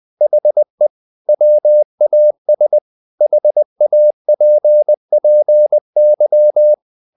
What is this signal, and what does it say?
Morse code, heard as a keyed tone, about 20 wpm, 600 Hz. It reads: HE WAS HAPPY
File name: the old man and the sea / the old man and the sea part 1 20wpm